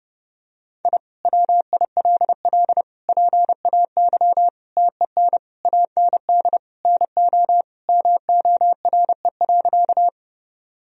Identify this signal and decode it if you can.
Morse code, heard as a keyed tone, about 30 wpm, 700 Hz. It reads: I WILL PAY TEN AND NO MORE.
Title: Sentences from Top 1000 Words - Review Part 2 - 30wpm